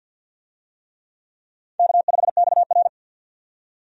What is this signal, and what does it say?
Morse code, heard as a keyed tone, about 52 wpm, 700 Hz. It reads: K5XR
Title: USA Call Signs - 4 Characters - 50wpm